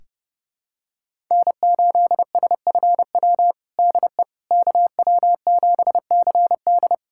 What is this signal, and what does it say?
Morse code, heard as a keyed tone, about 32 wpm, 700 Hz. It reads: N8SFW DE KW7CD
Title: USA Callsign DE Callsign - 5 Characters - 30wpm